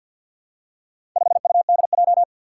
Morse code, heard as a keyed tone, about 53 wpm, 700 Hz. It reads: 5UDJ
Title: Four Challenging Letter-Number Combinations 50wpm